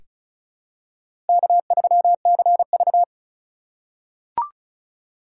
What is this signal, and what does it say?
Morse code, heard as a keyed tone, about 36 wpm, 700 Hz. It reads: K3CV
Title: USA Call Signs - 4 Characters - 2x Character Spacing - 35wpm